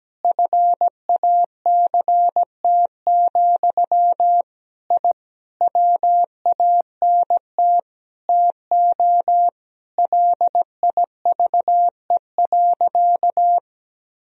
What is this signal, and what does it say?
Morse code, heard as a keyed tone, about 17 wpm, 700 Hz. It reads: FACT, I WANT TO LIVE.
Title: Sentences from Top 1000 Words - Review Part 3 - 17wpm